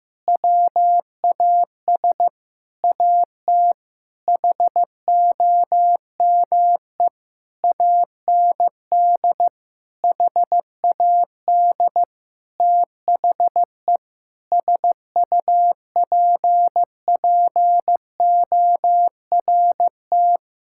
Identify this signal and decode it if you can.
Morse code, heard as a keyed tone, about 15 wpm, 700 Hz. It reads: WAS AT HOME AND HAD THE SUPPORT
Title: Sentences from Top 1000 Words - Review Part 5 - 15wpm